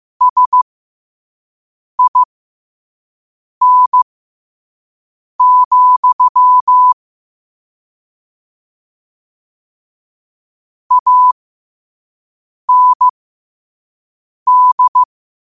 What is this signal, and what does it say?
Morse code, heard as a keyed tone, about 15 wpm, spaced as for 5 wpm, 1000 Hz. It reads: SIN, AND
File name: qotdfiles / 5wpm